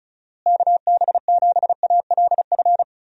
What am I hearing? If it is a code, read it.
Morse code, heard as a keyed tone, about 34 wpm, 700 Hz. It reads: KB7ALF
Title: USA Call Signs - 6 Characters - Repeated 3x - 35wpm